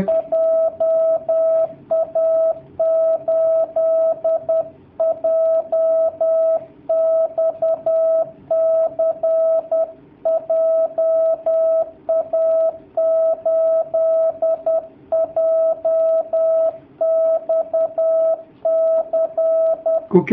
Morse code, heard as a keyed tone, about 10 wpm, 650 Hz. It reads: JA8JXC JA8JXC
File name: jh8cbh / 2019cw / sp10- w